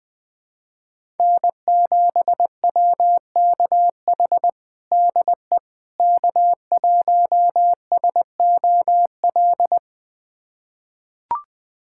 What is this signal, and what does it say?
Morse code, heard as a keyed tone, about 20 wpm, 700 Hz. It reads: N7WKH DE K1SOL
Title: USA Callsign DE Callsign - 5 Characters - 20wpm